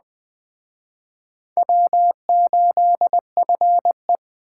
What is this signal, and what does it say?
Morse code, heard as a keyed tone, about 20 wpm, 700 Hz. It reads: W8FE
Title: Speed Racing - USA Call Signs 17wpm